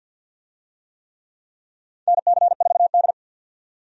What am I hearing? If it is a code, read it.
Morse code, heard as a keyed tone, about 47 wpm, 700 Hz. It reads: NC4D
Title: USA Call Signs - 4 Characters - 50wpm